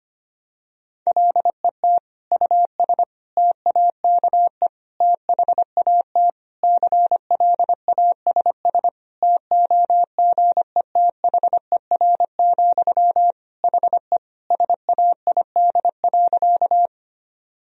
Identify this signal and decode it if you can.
Morse code, heard as a keyed tone, about 25 wpm, 700 Hz. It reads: LET US TAKE THAT CLASS TOGETHER, HE SAID.
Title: Sentences from Top 1000 Words - Review Part 2 - 25wpm